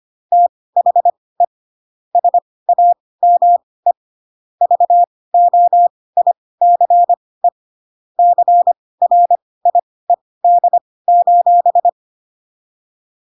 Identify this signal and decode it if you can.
Morse code, heard as a keyed tone, about 25 wpm, spaced as for 18 wpm, 700 Hz. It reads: THE SAME VOICE CRIED:
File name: Pinocchio / Pinocchio ch6 12 25 ik1hge